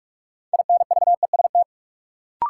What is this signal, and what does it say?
Morse code, heard as a keyed tone, about 44 wpm, 700 Hz. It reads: INVEST
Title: Sentences from Top 1100 Words - New Word Drill - 45wpm